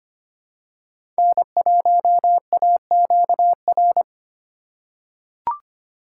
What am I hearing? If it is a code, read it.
Morse code, heard as a keyed tone, about 25 wpm, 700 Hz. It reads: N1AQR
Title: USA Call Signs - 5 Characters - 2x Character Spacing - 25wpm